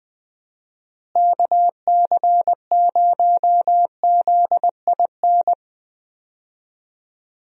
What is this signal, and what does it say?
Morse code, heard as a keyed tone, about 21 wpm, 700 Hz. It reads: KC0ZIN
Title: USA Call Signs - 6 Characters - Repeated 3x - 20wpm